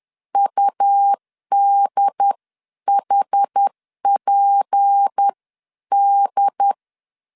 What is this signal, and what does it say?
Morse code, heard as a keyed tone, about 10 wpm, spaced as for 9 wpm, 800 Hz. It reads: UDHPD
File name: CW / DET / 1 6 1500C S C 10wpm 4252